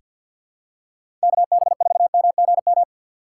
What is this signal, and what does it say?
Morse code, heard as a keyed tone, about 49 wpm, 700 Hz. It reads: KB4MGM
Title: USA Call Signs - 6 Characters - 6x Character Spacing - 50wpm